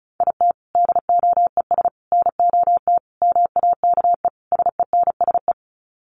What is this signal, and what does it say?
Morse code, heard as a keyed tone, about 34 wpm, 700 Hz. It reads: IT DOES NOT MAKE SENSE
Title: Sets of 5 Words Encore2 35wpm